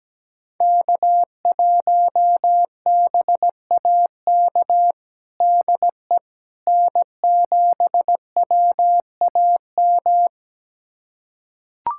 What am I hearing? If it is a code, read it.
Morse code, heard as a keyed tone, about 17 wpm, 700 Hz. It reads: K1BAK DE N7WAM
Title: USA Callsign DE Callsign - 5 Characters - 17wpm